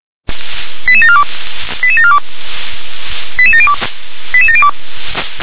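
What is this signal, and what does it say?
Five-tone selective call (ZVEI), heard as quick runs of five tones, tones of about 70 ms. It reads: 88742 88742 88702 88702
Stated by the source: Five-tone ZVEI selective call 88645, received via SDR.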